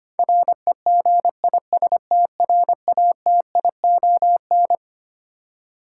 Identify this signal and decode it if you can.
Morse code, heard as a keyed tone, about 25 wpm, 700 Hz. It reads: REGISTRATION
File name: practice / 25wpm / Twelve Letter Words - 25wpm